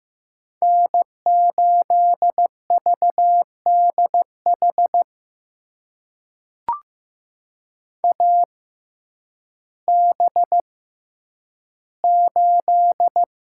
Morse code, heard as a keyed tone, about 15 wpm, spaced as for 9 wpm, 700 Hz. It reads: N8VDH AB8
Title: USA Call Signs - 5 Characters - 6x Character Spacing - 15wpm